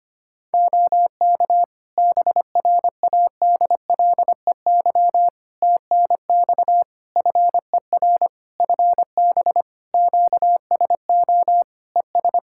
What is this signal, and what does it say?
Morse code, heard as a keyed tone, about 25 wpm, 700 Hz. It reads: OK BRADLEY TNX FER FB QSO ES
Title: Practice QSO - Exchange 4 - Ending - 25wpm